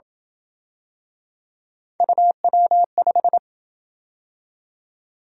Morse code, heard as a keyed tone, about 27 wpm, 700 Hz. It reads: UW5